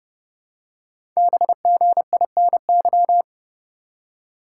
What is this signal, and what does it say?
Morse code, heard as a keyed tone, about 30 wpm, 700 Hz. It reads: BGINY
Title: Code Groups - 5 Letters - 30wpm